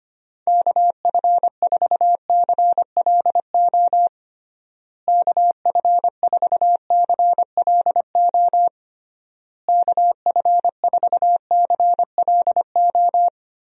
Morse code, heard as a keyed tone, about 25 wpm, 700 Hz. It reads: KF4CLO KF4CLO KF4CLO
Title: USA Call Signs - 6 Characters - Repeated 3x - 25wpm